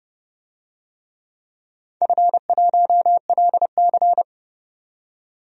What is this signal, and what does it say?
Morse code, heard as a keyed tone, about 30 wpm, 700 Hz. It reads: F1LC
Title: DXCC-100 Call Signs - 4 Characters - 30wpm